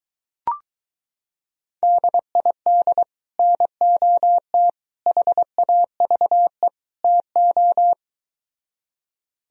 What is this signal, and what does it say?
Morse code, heard as a keyed tone, about 22 wpm, 700 Hz. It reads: DID NOT HAVE TO